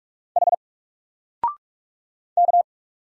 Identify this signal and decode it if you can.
Morse code, heard as a keyed tone, about 46 wpm, 700 Hz. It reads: H K
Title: Single Character (Letter:Number:Punctuation:Prosign) 45wpm